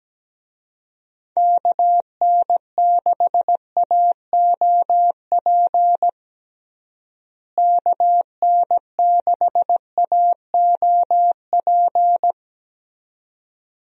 Morse code, heard as a keyed tone, about 17 wpm, 700 Hz. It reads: KN6AOP KN6AOP
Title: USA Call Signs - 6 Characters - Repeated 2x - 17wpm